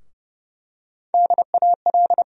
Morse code, heard as a keyed tone, about 30 wpm, 700 Hz. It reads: DAL
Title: USA Call Sign Postfixes - 3 Letters 30wpm